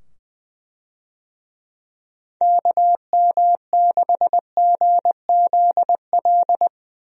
Morse code, heard as a keyed tone, about 20 wpm, 700 Hz. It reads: KM6GZL
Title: USA Call Signs - 6 Characters - 20wpm